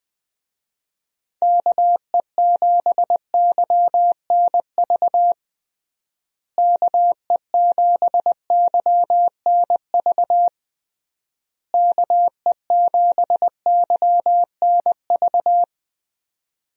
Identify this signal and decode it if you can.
Morse code, heard as a keyed tone, about 20 wpm, 700 Hz. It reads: KE7YNV KE7YNV KE7YNV
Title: USA Call Signs - 6 Characters - Repeated 3x - 20wpm